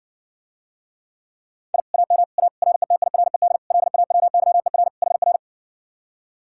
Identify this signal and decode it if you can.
Morse code, heard as a keyed tone, about 61 wpm, 700 Hz. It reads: I AM A RETIRED LAWYER HR